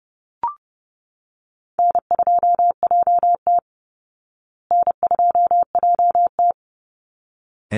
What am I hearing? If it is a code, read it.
Morse code, heard as a keyed tone, about 30 wpm, 700 Hz. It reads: N2JT N2JT
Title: Easy Call Signs Repeated 2x - 30wpm